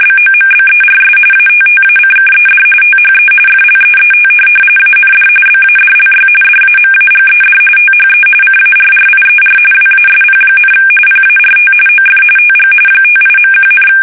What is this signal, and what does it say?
RTTY, about 45.45 baud. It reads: BOMBERS ATTACKED THE FLEET AND BADLY DAMAGED THE CARRIER WASP, PART OF TG 58.1,
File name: fsk850 / USS-Massachusetts